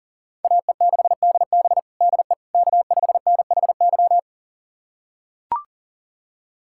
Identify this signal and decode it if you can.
Morse code, heard as a keyed tone, about 40 wpm, 700 Hz. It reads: AE6DB DE K5NHY
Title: USA Callsign DE Callsign - 5 Characters - 40wpm